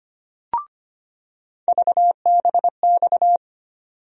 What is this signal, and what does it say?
Morse code, heard as a keyed tone, about 25 wpm, 700 Hz. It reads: VBX
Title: USA Call Sign Postfixes - 3 Letters 25wpm